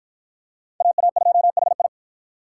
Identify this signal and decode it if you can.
Morse code, heard as a keyed tone, about 53 wpm, 700 Hz. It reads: AA2HI